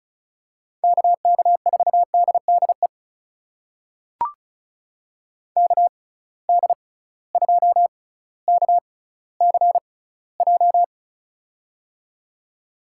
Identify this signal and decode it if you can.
Morse code, heard as a keyed tone, about 36 wpm, spaced as for 18 wpm, 690 Hz. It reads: KK4DDE KD2KCJ
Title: USA Call Signs - 6 Characters - 6x Character Spacing - 35wpm